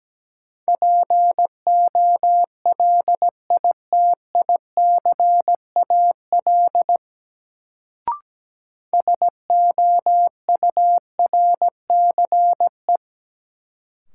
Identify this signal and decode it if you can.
Morse code, heard as a keyed tone, about 17 wpm, 700 Hz. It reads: POLITICAL SOURCE
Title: Sentences from Top 800 Words - New Word Drill - 17wpm